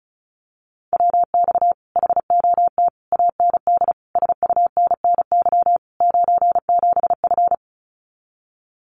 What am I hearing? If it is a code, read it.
Morse code, heard as a keyed tone, about 34 wpm, 700 Hz. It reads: WX HOT AND SUNNY 97F
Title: Common QSO Phrases Encore 35wpm